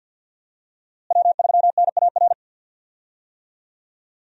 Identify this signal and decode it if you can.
Morse code, heard as a keyed tone, about 51 wpm, 700 Hz. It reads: W3NAR
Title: USA Call Signs - 5 Characters - Repeated 3x - 50wpm